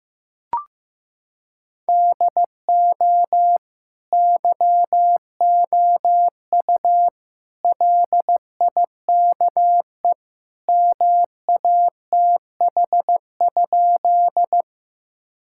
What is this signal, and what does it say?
Morse code, heard as a keyed tone, about 15 wpm, 700 Hz. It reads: DO YOU LIKE MATH?
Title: Sentences from Top 1000 Words - Review Part 6 - 15wpm